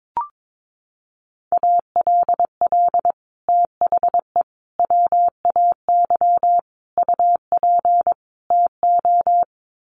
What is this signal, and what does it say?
Morse code, heard as a keyed tone, about 22 wpm, 700 Hz. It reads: ALL THE WAY UP TO
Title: Sets of 5 Words Encore 22wpm